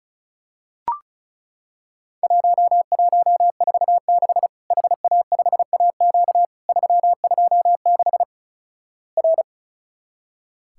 Morse code, heard as a keyed tone, about 36 wpm, 700 Hz. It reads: 1146 HA5AQ 326 R
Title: Worked All Europe DX Contest - QTC Exchange - 35wpm